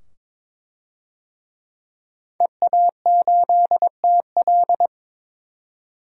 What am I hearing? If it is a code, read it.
Morse code, heard as a keyed tone, about 22 wpm, 700 Hz. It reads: EA8TL
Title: DXCC-100 Call Signs - 5 Characters - 22wpm